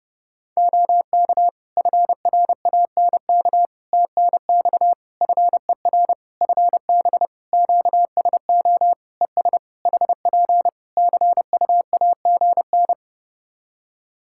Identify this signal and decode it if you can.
Morse code, heard as a keyed tone, about 30 wpm, 700 Hz. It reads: OK FRANK TNX FER FB QSO ES HP CUAGN
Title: Practice QSO - Exchange 4 - Ending - 30wpm